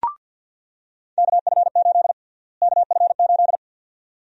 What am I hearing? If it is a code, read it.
Morse code, heard as a keyed tone, about 47 wpm, 700 Hz. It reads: KF8 KF8
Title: USA Call Sign Prefixes - Repeated 2x - 50wpm